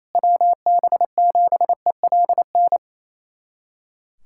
Morse code, heard as a keyed tone, about 28 wpm, 700 Hz. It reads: WB7ELN
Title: USA Callsign DE Callsign 28wpm